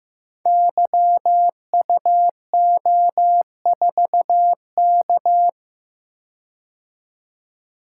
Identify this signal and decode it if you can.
Morse code, heard as a keyed tone, about 15 wpm, 700 Hz. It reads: YUO4K
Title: Code Groups - 5 Letter:Numbers - 15wpm